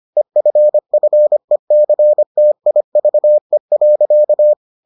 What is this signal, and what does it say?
Morse code, heard as a keyed tone, about 25 wpm, 600 Hz. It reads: EFFECTIVE.